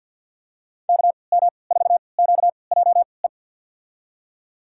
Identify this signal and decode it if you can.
Morse code, heard as a keyed tone, about 46 wpm, spaced as for 27 wpm, 700 Hz. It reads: KM4QJE
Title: USA Call Signs - 6 Characters - 3x Character Spacing - 50wpm